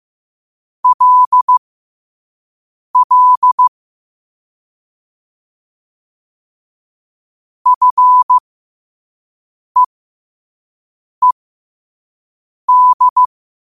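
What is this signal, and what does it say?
Morse code, heard as a keyed tone, about 15 wpm, spaced as for 5 wpm, 1000 Hz. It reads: LL FEED
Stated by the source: Morse code quote of the day sent at 5 wpm.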